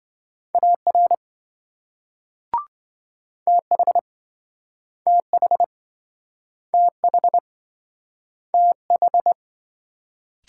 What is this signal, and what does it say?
Morse code, heard as a keyed tone, about 26 wpm, 700 Hz. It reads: AR TH TH TH TH